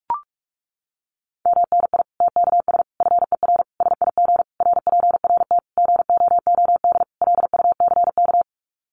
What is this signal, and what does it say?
Morse code, heard as a keyed tone, about 44 wpm, 700 Hz. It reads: MNI TKS FER SIG RPRT GOOD LUCK
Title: Common QSO Phrases 45wpm